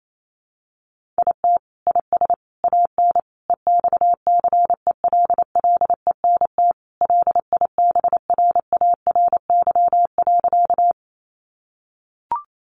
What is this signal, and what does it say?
Morse code, heard as a keyed tone, about 28 wpm, 700 Hz. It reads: IT IS AN EXCELLENT LIBRARY.
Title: Sentences from Top 1000 Words 28wpm